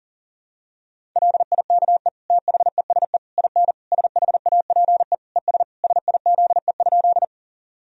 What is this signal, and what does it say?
Morse code, heard as a keyed tone, about 40 wpm, 700 Hz. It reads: LIKE THESE IN SHAPE ES SIZE?
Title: Sentences from Top 500 Words and Abbreviations 40wpm